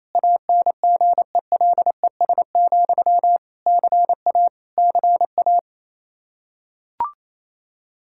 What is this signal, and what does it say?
Morse code, heard as a keyed tone, about 28 wpm, 700 Hz. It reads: ANGELES, CA CA
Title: K1USN SST QSO Practice 28wpm